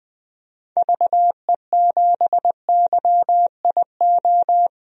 Morse code, heard as a keyed tone, about 20 wpm, 700 Hz. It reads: VE7YIO
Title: DXCC-100 Call Signs - 6 Characters - 20wpm